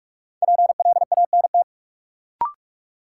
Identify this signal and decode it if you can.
Morse code, heard as a keyed tone, about 42 wpm, 700 Hz. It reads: PLANT
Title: Sentences from Top 1000 Words - Review Part 1 - New Word Drill - 45wpm